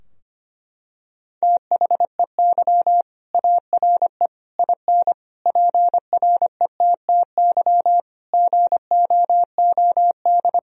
Morse code, heard as a keyed tone, about 25 wpm, 700 Hz. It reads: THEY ARE IN PRETTY GOOD
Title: Sentences from Top 1000 Words - Review Part 3 - 25wpm